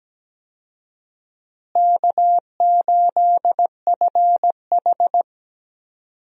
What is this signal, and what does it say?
Morse code, heard as a keyed tone, about 17 wpm, 700 Hz. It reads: K8FH
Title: USA Call Signs - 4 Characters - Repeated 3x - 17wpm